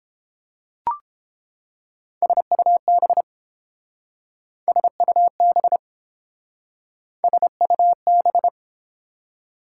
Morse code, heard as a keyed tone, about 30 wpm, 700 Hz. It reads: SUB SUB SUB